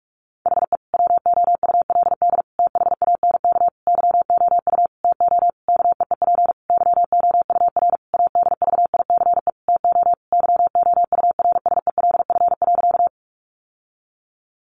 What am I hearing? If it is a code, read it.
Morse code, heard as a keyed tone, about 44 wpm, 700 Hz. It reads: HE WOULD THANK YOU TO KEEP YOUR ADVICE TO YOURSELF.
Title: Sentences from Top 1100 Words 45wpm